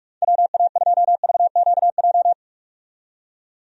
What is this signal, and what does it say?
Morse code, heard as a keyed tone, about 46 wpm, 700 Hz. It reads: WA2VQJ
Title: USA Call Signs - 6 Characters - 45wpm